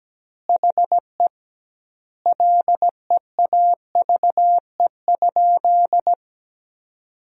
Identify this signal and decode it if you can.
Morse code, heard as a keyed tone, about 17 wpm, 700 Hz. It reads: HE LEAVE?
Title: 2x Word-Spacing - Sentences from Top 400 Words - 17wpm